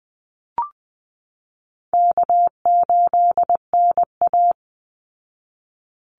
Morse code, heard as a keyed tone, about 20 wpm, 700 Hz. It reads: K8NA
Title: Easy Call Signs Repeated 2x - 20wpm